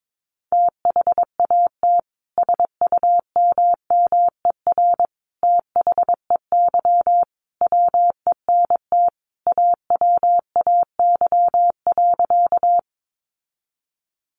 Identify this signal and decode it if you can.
Morse code, heard as a keyed tone, about 22 wpm, 700 Hz. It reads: THAT SUMMER THEY WENT AWAY.
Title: Sentences from Top 1100 Words 22wpm